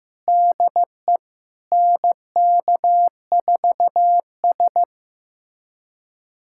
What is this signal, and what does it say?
Morse code, heard as a keyed tone, about 16 wpm, 700 Hz. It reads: DE NK4S
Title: USA Callsign DE Callsign 15wpm